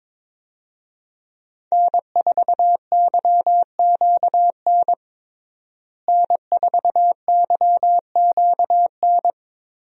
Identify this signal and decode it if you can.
Morse code, heard as a keyed tone, about 21 wpm, 700 Hz. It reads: N4YQN N4YQN
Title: USA Call Signs - 5 Characters - Repeated 3x - 22wpm